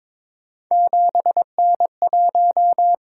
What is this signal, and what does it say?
Morse code, heard as a keyed tone, about 22 wpm, 700 Hz. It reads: 7N1
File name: practice / 22wpm / Three Letter-Number Combinations - 22wpm